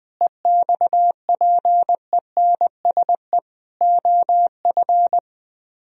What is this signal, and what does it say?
Morse code, heard as a keyed tone, about 20 wpm, 700 Hz. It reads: EXPENSE OF
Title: Speed-Racing - Sets of 4 Words 17wpm